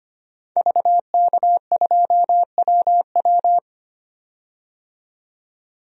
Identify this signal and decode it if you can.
Morse code, heard as a keyed tone, about 25 wpm, 700 Hz. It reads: VK2WW
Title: DXCC-100 Call Signs - 5 Characters - 25wpm